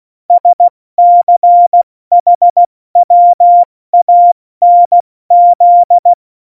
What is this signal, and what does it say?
Morse code, heard as a keyed tone, about 16 wpm, spaced as for 14 wpm, 700 Hz. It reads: SCHWANZ
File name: Morsekurs / Witze / Witz 25. Dezember, 70 F80